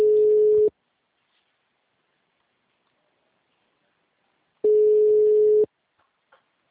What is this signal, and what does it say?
Ringback tone (Europe, 425 Hz).